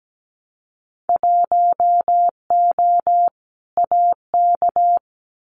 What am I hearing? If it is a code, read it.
Morse code, heard as a keyed tone, about 17 wpm, 700 Hz. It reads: 1O AK
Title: Winter Field Day Exchanges Encore 17wpm